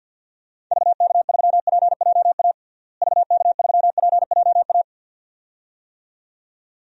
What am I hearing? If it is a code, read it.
Morse code, heard as a keyed tone, about 51 wpm, 700 Hz. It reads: VK3PJA VK3PJA
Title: DXCC-100 Call Signs - 6 Characters - Repeated 2x - 50wpm